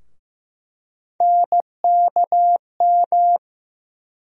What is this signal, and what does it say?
Morse code, heard as a keyed tone, about 15 wpm, 700 Hz. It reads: NKM
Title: USA Call Sign Postfixes - 3 Letters 15wpm